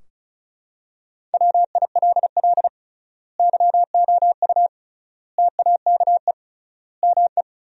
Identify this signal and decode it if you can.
Morse code, heard as a keyed tone, about 34 wpm, 700 Hz. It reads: WILL YOU TAKE ME
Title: 3x Word-Spacing - Sentences from Top 200 Words - 35wpm